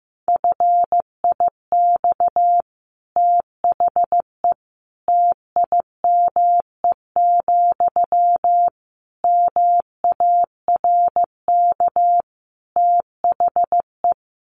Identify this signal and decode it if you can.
Morse code, heard as a keyed tone, about 15 wpm, 700 Hz. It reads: FIX THE TIME, MARK THE
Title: Sentences from Top 1400 Words 15wpm